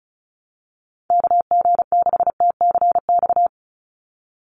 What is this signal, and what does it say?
Morse code, heard as a keyed tone, about 34 wpm, 700 Hz. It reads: KG6TCX
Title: USA Call Signs 35wpm